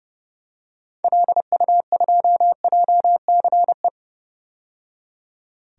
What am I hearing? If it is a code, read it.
Morse code, heard as a keyed tone, about 30 wpm, 700 Hz. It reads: LU2JCE
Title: DXCC-100 Call Signs - 6 Characters - 30wpm